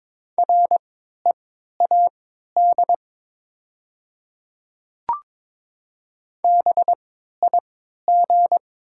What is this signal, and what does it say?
Morse code, heard as a keyed tone, about 22 wpm, spaced as for 12 wpm, 700 Hz. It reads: READ BIG
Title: 3x Character Spacing - Top 300 Words - 22wpm